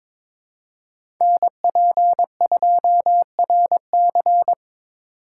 Morse code, heard as a keyed tone, about 23 wpm, 700 Hz. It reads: NP2RC
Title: USA Call Signs - 5 Characters - 4x Character Spacing - 22wpm